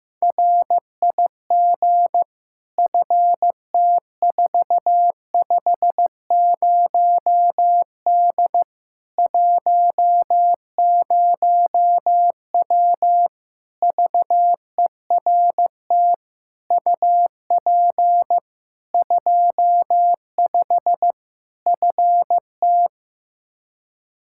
Morse code, heard as a keyed tone, about 15 wpm, 700 Hz. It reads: RIG FT450D 10W VERT UP 25 FT
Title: Practice QSO - Exchange 2 - Info - 15wpm